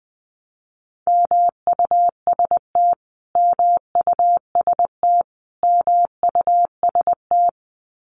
Morse code, heard as a keyed tone, about 19 wpm, 700 Hz. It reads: MUST MUST MUST
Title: Top 500 Words Repeated 20wpm